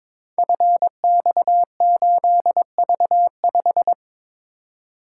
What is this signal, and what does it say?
Morse code, heard as a keyed tone, about 22 wpm, 700 Hz. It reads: FX8V5
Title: Code Groups - 5 Letter:Numbers - 22wpm